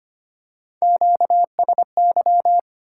Morse code, heard as a keyed tone, about 25 wpm, 700 Hz. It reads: QSY
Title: Top 180 QSO Elements 25wpm